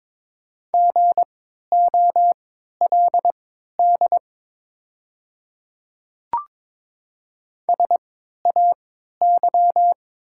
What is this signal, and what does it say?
Morse code, heard as a keyed tone, about 22 wpm, spaced as for 12 wpm, 690 Hz. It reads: GOLD SAY
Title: 3x Character Spacing - Top 500 Words - 22wpm